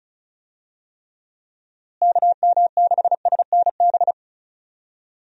Morse code, heard as a keyed tone, about 36 wpm, 700 Hz. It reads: KM6SNB
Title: USA Call Signs - 6 Characters - Repeated 2x - 35wpm